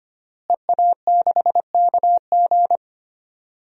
Morse code, heard as a keyed tone, about 25 wpm, 700 Hz. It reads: EA6KG
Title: DXCC-100 Call Signs - 5 Characters - Repeated 2x - 25wpm